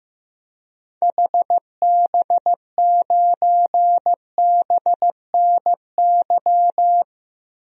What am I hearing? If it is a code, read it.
Morse code, heard as a keyed tone, about 15 wpm, 700 Hz. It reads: HB9BNY
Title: DXCC-100 Call Signs - 6 Characters - 15wpm